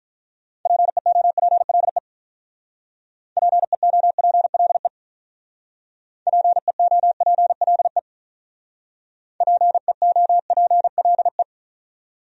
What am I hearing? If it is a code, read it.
Morse code, heard as a keyed tone, about 45 wpm, 700 Hz. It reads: PEOPLE PEOPLE PEOPLE PEOPLE